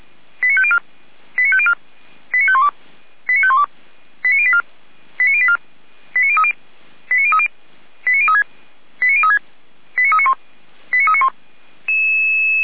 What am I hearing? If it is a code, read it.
Five-tone selective call (ZVEI), heard as quick runs of five tones, tones of about 70 ms. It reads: 89504 89504 89512 89512 89084 89084 89030 89030 89036 89036 89391 89391